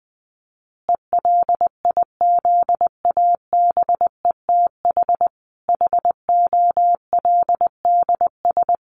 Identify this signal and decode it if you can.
Morse code, heard as a keyed tone, about 20 wpm, 700 Hz. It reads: ELIZABETH HOLDS
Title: Easy Phrases 20wpm